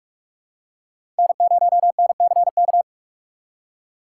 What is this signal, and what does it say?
Morse code, heard as a keyed tone, about 45 wpm, 700 Hz. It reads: N0NCK
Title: USA Call Signs - 5 Characters - 4x Character Spacing - 45wpm